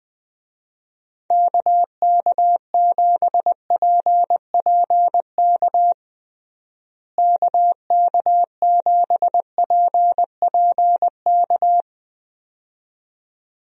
Morse code, heard as a keyed tone, about 21 wpm, 700 Hz. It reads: KK7PPK KK7PPK
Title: USA Call Signs - 6 Characters - Repeated 2x - 20wpm